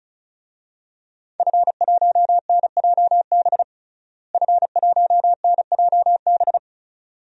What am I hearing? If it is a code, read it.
Morse code, heard as a keyed tone, about 36 wpm, 700 Hz. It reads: F1NJB F1NJB
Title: DXCC-100 Call Signs - 5 Characters - Repeated 2x - 35wpm